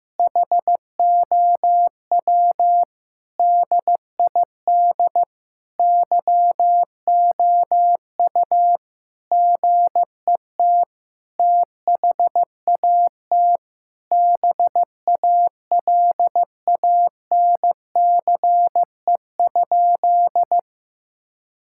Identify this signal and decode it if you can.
Morse code, heard as a keyed tone, about 15 wpm, 700 Hz. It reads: HOW DID YOU GET THAT BALANCE?
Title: Sentences from Top 1000 Words - Review Part 6 - 15wpm